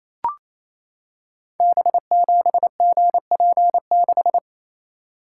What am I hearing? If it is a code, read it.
Morse code, heard as a keyed tone, about 27 wpm, 700 Hz. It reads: B7GP6
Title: Code Groups - 5 Letter:Numbers - 28wpm